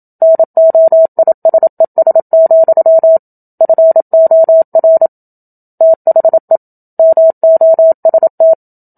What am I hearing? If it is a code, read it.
Morse code, heard as a keyed tone, about 27 wpm, 650 Hz. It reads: NOISES, FOR THE MOST
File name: Book Two / 01-Chapter 1 - 27 WPM